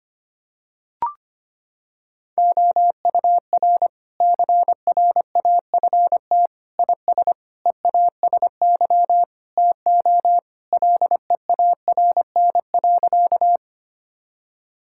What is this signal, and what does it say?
Morse code, heard as a keyed tone, about 25 wpm, 700 Hz. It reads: OUR CRAFT IS EASY TO LEARN.
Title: Sentences from Top 1000 Words - Review Part 6 - 25wpm